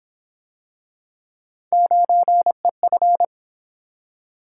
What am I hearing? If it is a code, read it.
Morse code, heard as a keyed tone, about 27 wpm, 700 Hz. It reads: 9EF
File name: Speed-Racing - Three Letter-Number Combinations - 22wpm